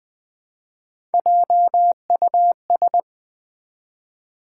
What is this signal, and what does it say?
Morse code, heard as a keyed tone, about 20 wpm, 700 Hz. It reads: JUS
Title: Build A Word - 1 Letter At A Time - 3 to 5-Letter Words - 20wpm